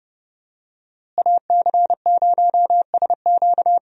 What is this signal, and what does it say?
Morse code, heard as a keyed tone, about 30 wpm, 700 Hz. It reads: AC0SQ
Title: USA Call Signs - 5 Characters - Repeated 3x - 30wpm